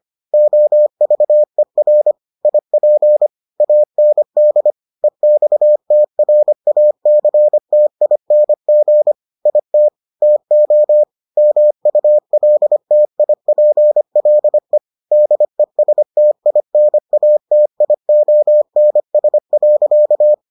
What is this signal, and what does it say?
Morse code, heard as a keyed tone, about 25 wpm, 600 Hz. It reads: OVER IP AND EXTRACTING IT TO MULTIPLE DESTINATIONS.